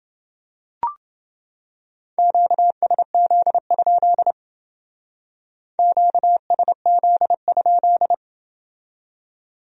Morse code, heard as a keyed tone, about 28 wpm, 700 Hz. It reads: QSZ? QSZ?